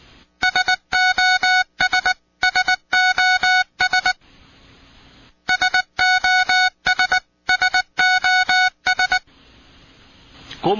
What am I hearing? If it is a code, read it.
Morse code, heard as a keyed tone, about 19 wpm, 1480 Hz, with speech. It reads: SOS SOS SOS SOS